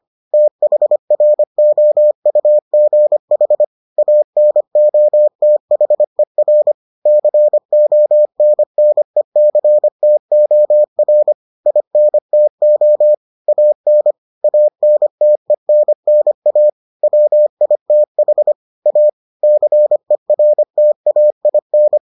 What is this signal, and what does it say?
Morse code, heard as a keyed tone, about 25 wpm, 600 Hz. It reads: THROUGH ANOTHER CONNECTOR INTO AN ANTENNA WITH A CERTAIN